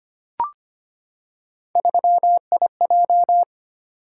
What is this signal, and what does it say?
Morse code, heard as a keyed tone, about 25 wpm, 700 Hz. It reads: 3IJ
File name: Three Letter-Number Combinations - 25wpm